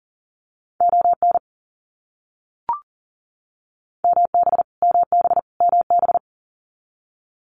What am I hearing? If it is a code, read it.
Morse code, heard as a keyed tone, about 40 wpm, 700 Hz. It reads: ON MB MB MB
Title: Canadian Provinces and Territories - Repeated 3x - 40wpm